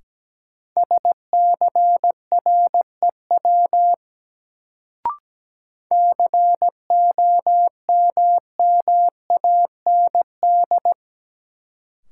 Morse code, heard as a keyed tone, about 17 wpm, 700 Hz. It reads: SCREW COMMAND
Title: Sentences from Top 1600 Words - New Word Drill Alt - 17wpm